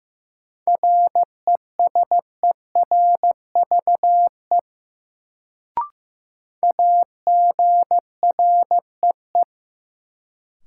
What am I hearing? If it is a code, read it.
Morse code, heard as a keyed tone, about 15 wpm, 700 Hz. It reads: RESERVE AGREE
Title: Sentences from Top 1000 Words - Review Part 6 - New Word Drill - 15wpm